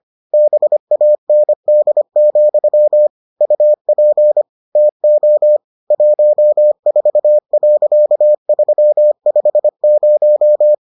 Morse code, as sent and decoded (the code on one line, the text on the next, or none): -... .- -. -.. --..-- / ..- .--. / - --- / .---- ....- .-.-.- ...-- ..... -----
BAND, UP TO 14.350